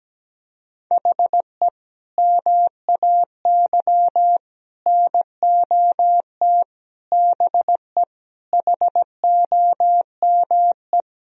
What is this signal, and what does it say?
Morse code, heard as a keyed tone, about 17 wpm, 700 Hz. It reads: HE MAY NOT BE HOME
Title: Sentences from Top 1000 Words - Review Part 2 - 17wpm